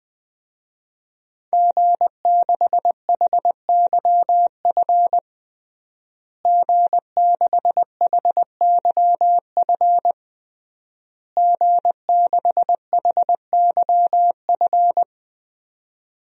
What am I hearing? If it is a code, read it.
Morse code, heard as a keyed tone, about 20 wpm, 700 Hz. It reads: G6HYF G6HYF G6HYF
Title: DXCC-100 Call Signs - 5 Characters - Repeated 3x - 20wpm